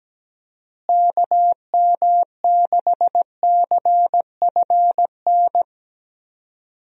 Morse code, heard as a keyed tone, about 17 wpm, 700 Hz. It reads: KM6CFN
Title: USA Call Signs - 6 Characters - Repeated 2x - 17wpm